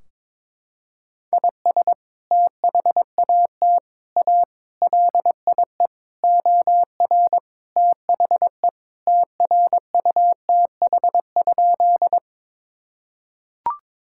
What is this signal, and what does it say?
Morse code, heard as a keyed tone, about 22 wpm, 700 Hz. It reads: IS THAT A LIE OR THE TRUTH?
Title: Sentences from Top 1600 Words 22wpm